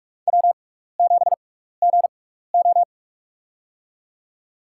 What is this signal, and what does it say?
Morse code, heard as a keyed tone, about 44 wpm, spaced as for 15 wpm, 700 Hz. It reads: W7GO
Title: USA Call Signs - 6x Character Spacing - 45wpm